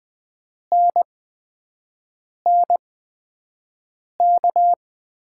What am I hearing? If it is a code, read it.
Morse code, heard as a keyed tone, about 20 wpm, spaced as for 5 wpm, 700 Hz. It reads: NNK
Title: USA Call Signs - 5 Characters - 8x Character Spacing - 20wpm